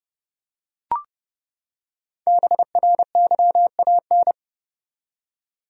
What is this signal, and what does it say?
Morse code, heard as a keyed tone, about 29 wpm, 700 Hz. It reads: BRYAN
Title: USA Common Names - Max 5 Letters - 30wpm